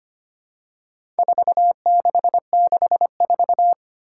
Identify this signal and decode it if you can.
Morse code, heard as a keyed tone, about 25 wpm, 700 Hz. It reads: 4664
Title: Four Challenging Letter-Number Combinations 25wpm